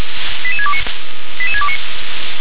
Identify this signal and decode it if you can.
Five-tone selective call (ZVEI), heard as quick runs of five tones, tones of about 70 ms. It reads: 88629 88629